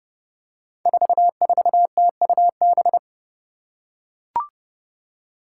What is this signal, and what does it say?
Morse code, heard as a keyed tone, about 30 wpm, 700 Hz. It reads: 44TUB
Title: Five Challenging Letter-Number Combinations 30wpm